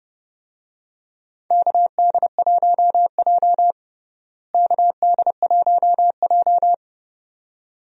Morse code, heard as a keyed tone, about 31 wpm, 700 Hz. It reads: KD1J KD1J
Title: USA Call Signs - 4 Characters - Repeated 2x - 30wpm